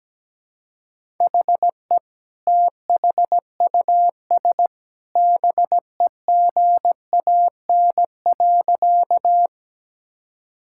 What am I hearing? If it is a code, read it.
Morse code, heard as a keyed tone, about 17 wpm, 700 Hz. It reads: HE THUS BEGAN.
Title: Sentences from Top 1000 Words - Review Part 5 - 17wpm